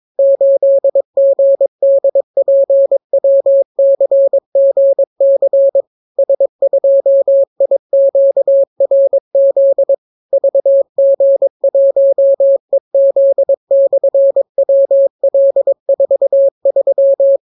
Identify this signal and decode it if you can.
Morse code, heard as a keyed tone, about 22 wpm, 550 Hz. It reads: 8GDPWCGC S2IQRZ VG1EZ/WL43